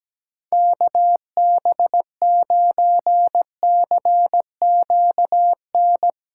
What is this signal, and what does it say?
Morse code, heard as a keyed tone, about 17 wpm, 700 Hz. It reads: KB9CQN
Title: USA Callsign DE Callsign - 6 Characters - 17wpm